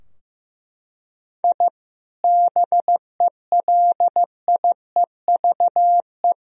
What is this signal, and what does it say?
Morse code, heard as a keyed tone, about 15 wpm, 700 Hz. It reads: I BELIEVE
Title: Sentences from Top 1000 Words - Review Part 3 - 15wpm